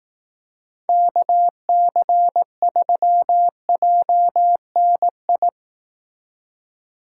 Morse code, heard as a keyed tone, about 18 wpm, 700 Hz. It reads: KC3JNI
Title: Speed Racing - USA 2x3 Call Signs 15wpm